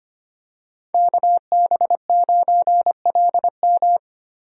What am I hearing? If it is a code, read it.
Morse code, heard as a keyed tone, about 25 wpm, 700 Hz. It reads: KB9LM
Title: USA Call Signs - 5 Characters - 25wpm